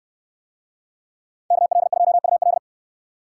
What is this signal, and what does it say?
Morse code, heard as a keyed tone, about 63 wpm, 700 Hz. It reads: KL2UL